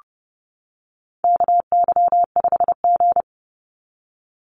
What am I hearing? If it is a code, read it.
Morse code, heard as a keyed tone, about 30 wpm, 700 Hz. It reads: KY5G
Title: USA Call Signs 30wpm